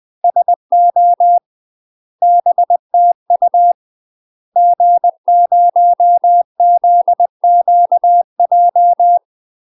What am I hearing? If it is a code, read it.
Morse code, heard as a keyed tone, about 20 wpm, 700 Hz. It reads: SO BTU G0ZQJ
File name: UK MASTER QSOs TRAINING / 20WPM / UK MASTER QSOs 0028